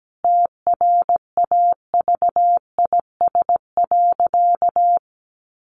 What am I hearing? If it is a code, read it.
Morse code, heard as a keyed tone, about 17 wpm, 700 Hz. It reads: TRAVIS.
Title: A Sound Of Thunder 17wpm